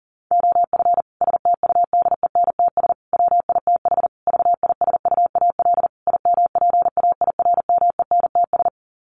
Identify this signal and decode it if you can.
Morse code, heard as a keyed tone, about 40 wpm, 700 Hz. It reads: OF STUDENTS WITH VISUAL IMPAIRMENTS
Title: Sets of 5 Words Encore 40wpm